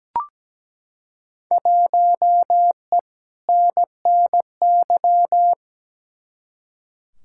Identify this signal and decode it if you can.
Morse code, heard as a keyed tone, about 17 wpm, 700 Hz. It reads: 1E NNY
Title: Field Day Exchanges - 17wpm